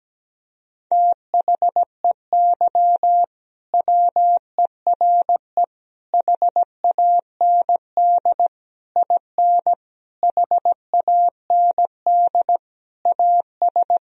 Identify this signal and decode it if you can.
Morse code, heard as a keyed tone, about 17 wpm, 700 Hz. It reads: THEY WERE HAND IN HAND AS
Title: Sentences from Top 1000 Words - Review Part 2 - 17wpm